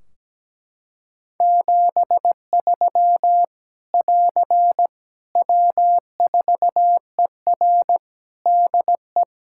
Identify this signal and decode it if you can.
Morse code, heard as a keyed tone, about 17 wpm, 700 Hz. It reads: 73 <AR> W4ER DE
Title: Practice QSO - Exchange 4 - Ending - 17wpm